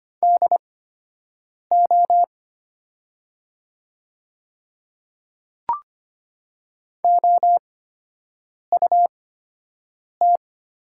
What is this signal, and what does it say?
Morse code, heard as a keyed tone, about 25 wpm, spaced as for 7 wpm, 700 Hz. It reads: DO OUT
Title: 8x Character Spacing - Top 100 Words - 25wpm